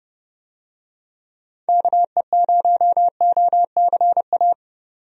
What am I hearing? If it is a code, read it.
Morse code, heard as a keyed tone, about 32 wpm, 700 Hz. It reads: KE0OCA
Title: USA Call Signs - 6 Characters - 30wpm